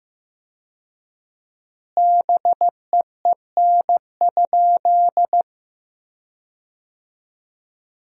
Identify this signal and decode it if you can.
Morse code, heard as a keyed tone, about 16 wpm, 700 Hz. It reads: BEEN?
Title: 4x Word-Spacing - Sentences from Top 200 Words - 15wpm